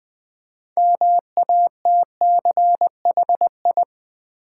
Morse code, heard as a keyed tone, about 20 wpm, 700 Hz. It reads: MATCHI
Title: Build A Word - 3 Letters At A Time - 8-Letter Words - 20wpm